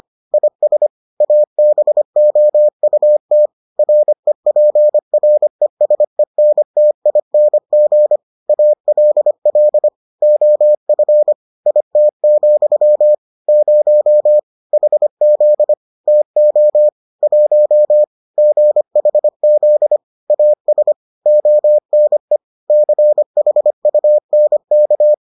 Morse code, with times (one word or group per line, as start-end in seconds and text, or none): 0.3-0.9 IS
1.2-3.5 ABOUT
3.8-8.2 REPRESENTING
8.5-9.9 ALL
10.2-11.3 OF
11.7-13.1 IT,
13.5-14.4 0
14.7-15.7 HZ
16.1-16.9 TO
17.2-18.0 1
18.4-20.0 GHZ
20.3-20.9 AS
21.3-22.4 ONE
22.7-25.1 CHUNK